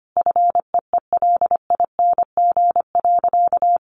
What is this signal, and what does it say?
Morse code, heard as a keyed tone, about 25 wpm, 700 Hz. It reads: FEELING.
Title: Sentences from Top 1200 Words 25wpm